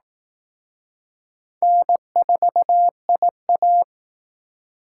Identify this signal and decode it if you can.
Morse code, heard as a keyed tone, about 19 wpm, 700 Hz. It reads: N4IA
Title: Speed Racing - USA Call Signs 15wpm